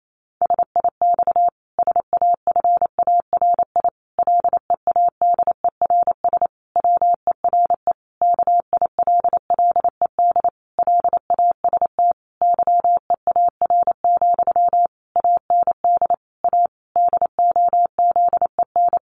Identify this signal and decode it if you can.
Morse code, heard as a keyed tone, about 28 wpm, 700 Hz. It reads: SIX SAFARI LEADERS WERE KILLED LAST YEAR, AND A DOZEN
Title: A Sound Of Thunder 28wpm